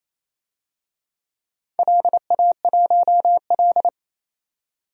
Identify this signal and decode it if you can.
Morse code, heard as a keyed tone, about 27 wpm, 700 Hz. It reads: LA1L
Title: DXCC-100 Call Signs - 4 Characters - 28wpm